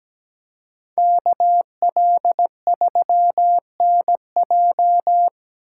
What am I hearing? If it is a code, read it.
Morse code, heard as a keyed tone, about 20 wpm, 700 Hz. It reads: KL3NJ
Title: Speed Racing - USA 2x2 Call Signs 17wpm